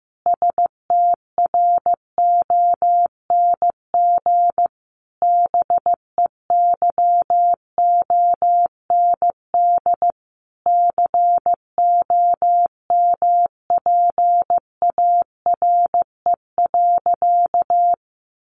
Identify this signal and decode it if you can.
Morse code, heard as a keyed tone, about 15 wpm, 700 Hz. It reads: STRONG BEYOND COMPARE.
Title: Sentences from Top 1300 Words 15wpm